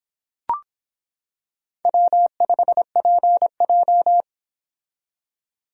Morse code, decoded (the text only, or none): W5PJ